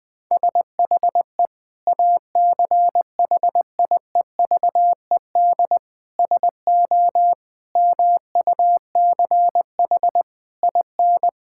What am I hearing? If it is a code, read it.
Morse code, heard as a keyed tone, about 20 wpm, 700 Hz. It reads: SHE ACHIEVED SO MUCH IN
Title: Sentences with 7-Word Phrases - Spoken Beginning - 20wpm